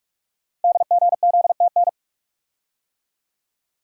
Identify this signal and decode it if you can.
Morse code, heard as a keyed tone, about 44 wpm, 700 Hz. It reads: DGZTN
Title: Code Groups - 5 Letter:Numbers - 45wpm